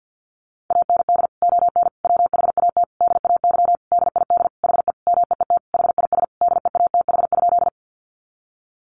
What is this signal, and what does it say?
Morse code, heard as a keyed tone, about 50 wpm, 700 Hz. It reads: AND ON WHAT DAY DID HE MEET HIS DEATH?
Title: Sentences from Top 1100 Words 50wpm